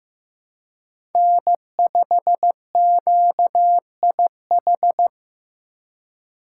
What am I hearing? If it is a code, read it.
Morse code, heard as a keyed tone, about 15 wpm, 700 Hz. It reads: N5QIH
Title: USA Call Signs - 8x Character Spacing - 15wpm